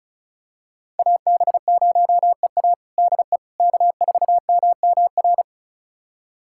Morse code, heard as a keyed tone, about 36 wpm, 700 Hz. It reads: AB0EA DE K4MMR
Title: USA Callsign DE Callsign - 5 Characters - 35wpm